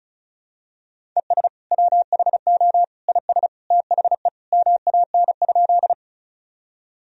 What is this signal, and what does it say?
Morse code, heard as a keyed tone, about 34 wpm, 700 Hz. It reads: ES WHO IS THE MAN?
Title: Sentences from Top 200 Words and Abbreviations 35wpm